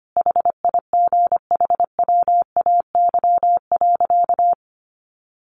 Morse code, heard as a keyed tone, about 25 wpm, 700 Hz. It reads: HIGHWAY.
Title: Sentences from Top 1400 Words 25wpm